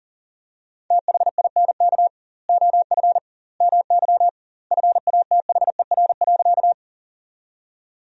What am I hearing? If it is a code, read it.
Morse code, heard as a keyed tone, about 40 wpm, 700 Hz. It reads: THINK OF MY FATHER.
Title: 2x Word-Spacing - Sentences from Top 300 Words - 40wpm